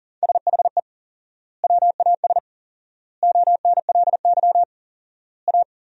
Morse code, heard as a keyed tone, about 40 wpm, 700 Hz. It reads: SHE WAS ONLY A
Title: 4x Word-Spacing - Sentences from Top 400 Words - 40wpm